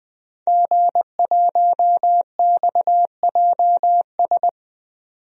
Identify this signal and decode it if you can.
Morse code, heard as a keyed tone, about 20 wpm, 700 Hz. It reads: G1XJS
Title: DXCC-100 Call Signs - 5 Characters - 20wpm